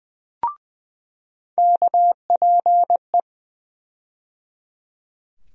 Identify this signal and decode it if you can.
Morse code, heard as a keyed tone, about 20 wpm, 700 Hz. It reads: KPE
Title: USA Call Sign Postfixes - 3 Letters 20wpm